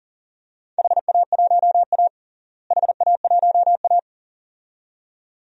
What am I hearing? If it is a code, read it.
Morse code, heard as a keyed tone, about 37 wpm, 700 Hz. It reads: HA1A HA1A
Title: DXCC-100 Call Signs - 4 Characters - Repeated 2x - 40wpm